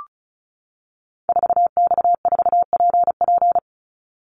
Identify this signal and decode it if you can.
Morse code, heard as a keyed tone, about 34 wpm, 700 Hz. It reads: 4X4PP
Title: Call Signs - 35wpm